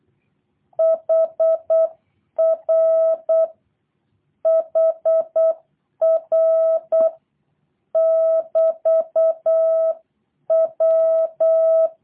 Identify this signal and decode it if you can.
Morse code, heard as a keyed tone, about 8 wpm, 650 Hz. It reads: HR HR <BT> W
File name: jh8cbh / 2023cw / review1-4